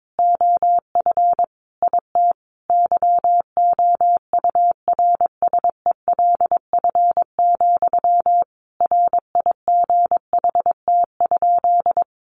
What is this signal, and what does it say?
Morse code, heard as a keyed tone, about 22 wpm, 700 Hz. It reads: OF IT YOURSELF, RIGHT?
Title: Sentences from Top 1200 Words 22wpm